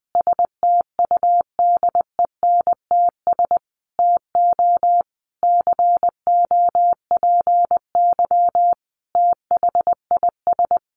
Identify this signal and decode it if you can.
Morse code, heard as a keyed tone, about 20 wpm, 700 Hz. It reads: STUDENTS TO COPY THIS
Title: Additional Easy Phrases 20wpm